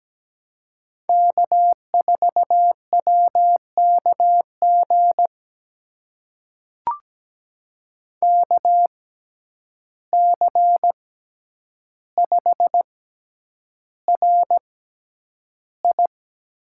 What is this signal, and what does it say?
Morse code, heard as a keyed tone, about 17 wpm, spaced as for 9 wpm, 700 Hz. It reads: K4WKG KC5RI
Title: USA Call Signs - 6x Character Spacing - 17wpm